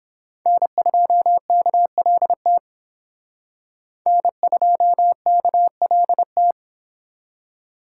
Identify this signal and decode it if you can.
Morse code, heard as a keyed tone, about 28 wpm, 700 Hz. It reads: N2KLT N2KLT